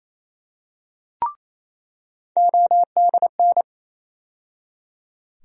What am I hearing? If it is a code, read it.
Morse code, heard as a keyed tone, about 28 wpm, 700 Hz. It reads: ODN
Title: USA Call Sign Postfixes - 3 Letters 28wpm